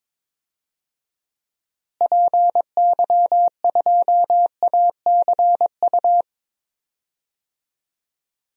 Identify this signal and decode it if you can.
Morse code, heard as a keyed tone, about 22 wpm, 700 Hz. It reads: PY2ACU
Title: DXCC-100 Call Signs - 6 Characters - Repeated 3x - 22wpm